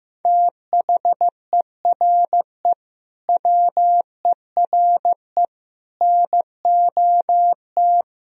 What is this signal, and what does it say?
Morse code, heard as a keyed tone, about 15 wpm, 700 Hz. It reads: THERE WERE NOT
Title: Sentences from Top 1000 Words - Review Part 1 - 15wpm